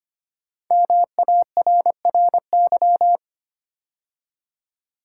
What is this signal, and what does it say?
Morse code, heard as a keyed tone, about 25 wpm, 700 Hz. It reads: MARRY
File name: 5 Letter Words - First 3 Letters Given - 25wpm